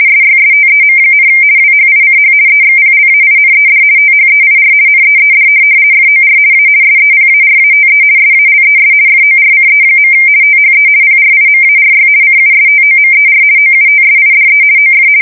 RTTY, about 50 baud. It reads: SYSTEM, WHICH EXPLAINED THEIR NEED FOR A SEPARATE SALES FORCE. THE PRIMARY CUSTOMER OUTSIDE OF
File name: fsk170-50B / teletype-corp